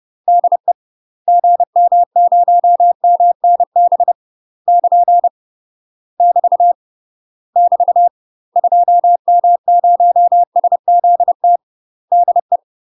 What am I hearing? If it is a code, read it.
Morse code, heard as a keyed tone, about 30 wpm, 700 Hz. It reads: DE GM0MNB <KN> <BT> <BT> 2M0SZT DE
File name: UK MASTER QSOs TRAINING / 30WPM / UK MASTER QSOs 0003